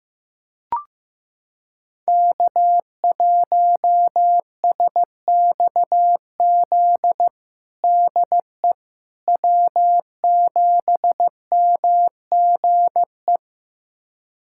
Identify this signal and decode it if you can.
Morse code, heard as a keyed tone, about 15 wpm, 700 Hz. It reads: K1SXZ DE W7MGE
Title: USA Callsign DE Callsign - 5 Characters - 15wpm